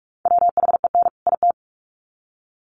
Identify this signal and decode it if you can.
Morse code, heard as a keyed tone, about 44 wpm, 700 Hz. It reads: WHEN IT
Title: Sets of 2 Words 45wpm